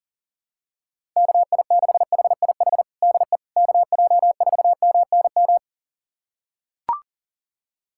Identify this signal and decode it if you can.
Morse code, heard as a keyed tone, about 40 wpm, 700 Hz. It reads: KI6HIH DE KJ4MNM
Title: USA Callsign DE Callsign 40wpm